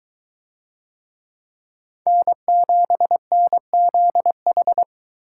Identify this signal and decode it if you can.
Morse code, heard as a keyed tone, about 22 wpm, 700 Hz. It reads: N7NZH